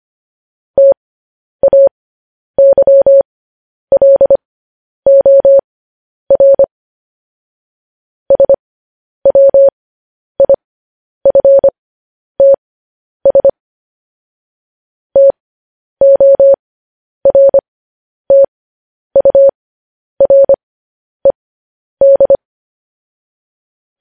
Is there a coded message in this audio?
Yes: Morse code from 0.8 to 22.4 s, heard as a keyed tone, about 25 wpm, spaced as for 10 wpm, 550 Hz. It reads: TAYLOR SWIFTS TORTURED